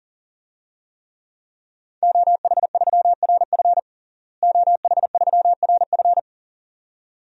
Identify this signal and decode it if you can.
Morse code, heard as a keyed tone, about 40 wpm, 700 Hz. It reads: OH3RF OH3RF
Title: DXCC-100 Call Signs - 5 Characters - Repeated 2x - 40wpm